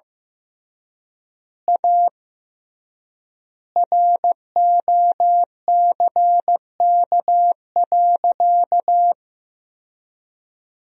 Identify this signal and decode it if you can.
Morse code, heard as a keyed tone, about 16 wpm, 700 Hz. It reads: A ROCK.
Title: 3x Word-Spacing - Sentences from Top 400 Words - 15wpm